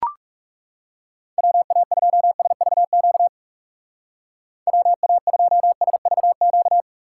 Morse code, heard as a keyed tone, about 36 wpm, 700 Hz. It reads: WA2SVQ WA2SVQ
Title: Speed Racing - USA 2x3 Call Signs 30wpm